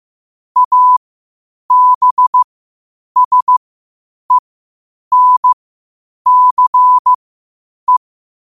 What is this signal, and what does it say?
Morse code, heard as a keyed tone, about 15 wpm, spaced as for 8 wpm, 1000 Hz. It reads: ABSENCE
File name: qotdfiles / 8wpm